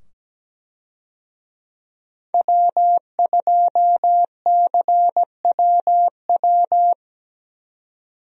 Morse code, heard as a keyed tone, about 17 wpm, 700 Hz. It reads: W2CWW
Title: USA Call Signs - 5 Characters - 17wpm